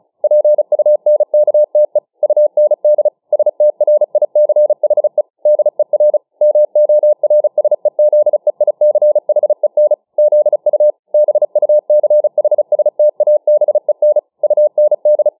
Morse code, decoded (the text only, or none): PUNKTE UND STRICHE DER MORSEZEICHEN ZU BUCHSTABEN UND